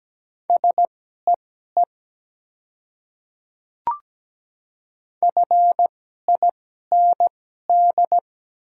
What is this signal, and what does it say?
Morse code, heard as a keyed tone, about 17 wpm, spaced as for 12 wpm, 690 Hz. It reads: SEE FIND
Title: 2x Character Spacing - Top 100 Words - 17wpm